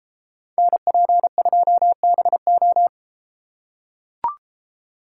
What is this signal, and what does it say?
Morse code, heard as a keyed tone, about 32 wpm, 700 Hz. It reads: NP2BO